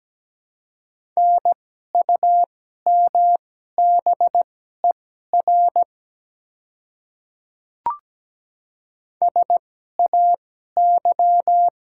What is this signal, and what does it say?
Morse code, heard as a keyed tone, about 17 wpm, spaced as for 12 wpm, 700 Hz. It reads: NUMBER SAY
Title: 2x Character Spacing - Top 200 Words - 17wpm